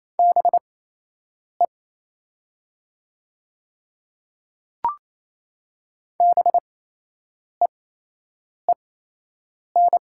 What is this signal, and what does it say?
Morse code, heard as a keyed tone, about 27 wpm, spaced as for 8 wpm, 700 Hz. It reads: BE BEEN
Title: 8x Character Spacing - Top 100 Words - 28wpm